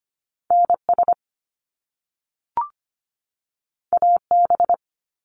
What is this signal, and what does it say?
Morse code, heard as a keyed tone, about 25 wpm, 700 Hz. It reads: NS AB
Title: Canadian Provinces and Territories 25wpm